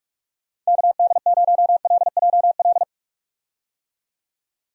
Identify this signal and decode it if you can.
Morse code, heard as a keyed tone, about 45 wpm, 700 Hz. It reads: KD0LJL
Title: USA Call Signs - 6 Characters - 45wpm